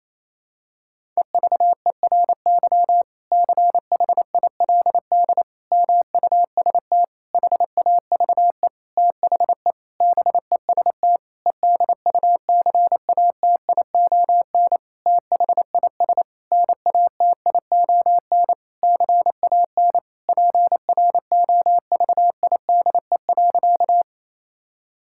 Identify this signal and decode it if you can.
Morse code, heard as a keyed tone, about 28 wpm, 700 Hz. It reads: EVERY CHILD MUST HAVE THE BEST EDUCATION THIS NATION CAN PROVIDE.
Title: Sentences from Top 1000 Words - Review Part 6 - 28wpm